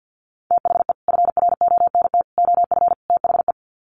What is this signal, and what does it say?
Morse code, heard as a keyed tone, about 50 wpm, 700 Hz. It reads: THE FRONT OF THE
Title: Sets of 4 Words 50wpm